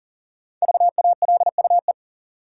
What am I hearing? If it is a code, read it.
Morse code, heard as a keyed tone, about 40 wpm, 700 Hz. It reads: VALUE
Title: Sentences from Top 600 Words - New Word Drill - 40wpm